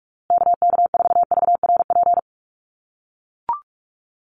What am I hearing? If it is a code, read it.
Morse code, heard as a keyed tone, about 44 wpm, 700 Hz. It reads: KK4VRP
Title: USA Call Signs Repeated 3x 45wpm